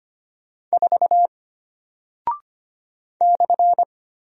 Morse code, heard as a keyed tone, about 25 wpm, 700 Hz. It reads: 4 /
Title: Single Character (Letter:Number:Punctuation:Prosign) Alt 25wpm